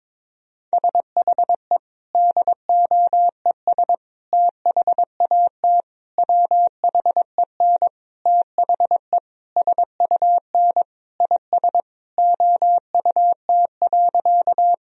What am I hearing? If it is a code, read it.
Morse code, heard as a keyed tone, about 21 wpm, 700 Hz. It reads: SHE DOES THAT WHEN THE SUN IS OUT.
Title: Sentences from Top 1000 Words - Review Part 1 - 22wpm